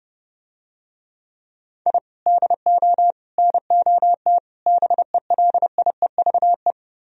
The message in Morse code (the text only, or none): I DO NOT BELIEVE